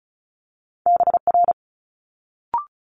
Morse code, heard as a keyed tone, about 36 wpm, 700 Hz. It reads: BR
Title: Multi-letter Phonemes Quick Fire 35wpm